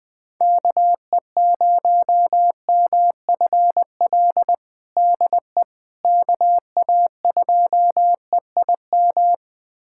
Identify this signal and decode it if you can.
Morse code, heard as a keyed tone, about 20 wpm, 700 Hz. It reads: KE0MFL DE KA2EIM
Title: USA Callsign DE Callsign - 6 Characters - 20wpm